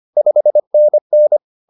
Morse code, heard as a keyed tone, about 25 wpm, 600 Hz. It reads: 5NN
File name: POTA107 25